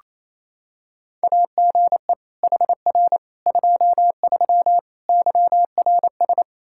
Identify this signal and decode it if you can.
Morse code, heard as a keyed tone, about 28 wpm, 700 Hz. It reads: AGE HR 23 YRS
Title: K1USN SST QSO Practice 28wpm